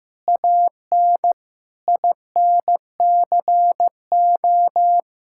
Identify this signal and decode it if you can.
Morse code, heard as a keyed tone, about 16 wpm, 700 Hz. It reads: AN INCO
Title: Sentences from Top 1000 Words - Review Part 5 - 15wpm